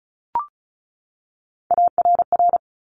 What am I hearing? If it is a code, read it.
Morse code, heard as a keyed tone, about 36 wpm, 700 Hz. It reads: ARR
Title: Top Three Letter Combinations in Words 35wpm